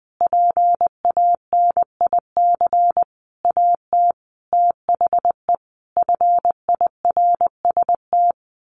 Morse code, heard as a keyed tone, about 20 wpm, 700 Hz. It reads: PANIC AT THE FIRST
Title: A Sound Of Thunder 20wpm